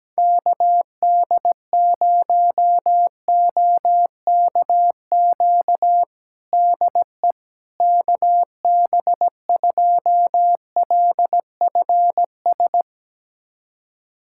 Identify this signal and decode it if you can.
Morse code, heard as a keyed tone, about 17 wpm, 700 Hz. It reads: KD0OKQ DE KB2LFS
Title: USA Callsign DE Callsign - 6 Characters - 17wpm